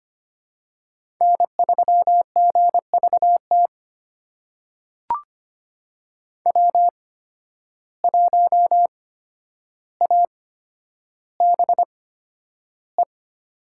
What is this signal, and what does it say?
Morse code, heard as a keyed tone, about 25 wpm, spaced as for 11 wpm, 700 Hz. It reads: N3GVT W1ABE
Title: USA Call Signs - 5 Characters - 8x Character Spacing - 25wpm